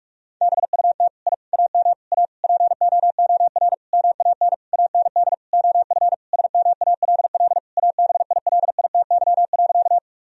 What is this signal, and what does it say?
Morse code, heard as a keyed tone, about 46 wpm, 700 Hz. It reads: BUT I AM A POOR MAN AND OF SMALL ABILITY.
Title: Sentences from Top 1000 Words - Review Part 3 - 45wpm